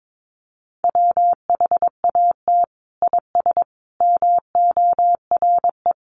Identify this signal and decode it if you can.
Morse code, heard as a keyed tone, about 22 wpm, 700 Hz. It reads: WHAT IS MORE
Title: Sentences from Top 1400 Words 22wpm